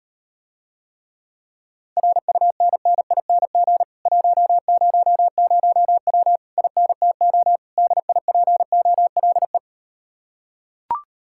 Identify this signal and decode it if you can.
Morse code, heard as a keyed tone, about 38 wpm, 700 Hz. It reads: RUNNING 100W INTO DIPOLE